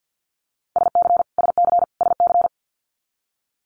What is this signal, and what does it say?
Morse code, heard as a keyed tone, about 50 wpm, 700 Hz. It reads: SC SC SC
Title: Multi-letter Phonemes Repeated 3x 50wpm